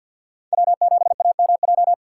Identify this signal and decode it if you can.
Morse code, heard as a keyed tone, about 52 wpm, 700 Hz. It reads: W7AMJ
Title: USA Call Signs - 5 Characters - 2x Character Spacing - 50wpm